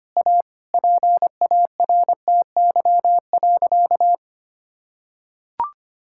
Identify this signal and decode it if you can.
Morse code, heard as a keyed tone, about 25 wpm, 700 Hz. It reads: A PARTY.
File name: practice / 25wpm / Sentences with 4-Word Phrases - 25wpm